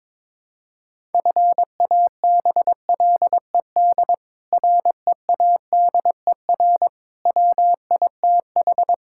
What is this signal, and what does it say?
Morse code, heard as a keyed tone, about 21 wpm, 700 Hz. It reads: FABLED READER WITH
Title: Pangram Challenge v2 - 22wpm